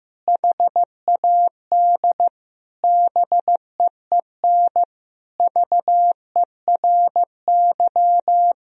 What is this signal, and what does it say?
Morse code, heard as a keyed tone, about 14 wpm, 700 Hz. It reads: HAD BEEN VERY
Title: Sentences from Top 1000 Words - Review Part 2 - 15wpm